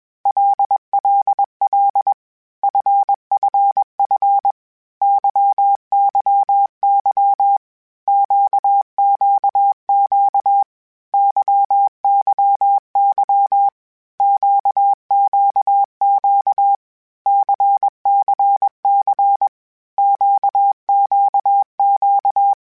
Morse code, heard as a keyed tone, about 21 wpm, 800 Hz. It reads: LLL FFF YYY QQQ YYY QQQ CCC QQQ
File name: Basic Sets / Basic Letter Groups Collective7 / LFCYQ@20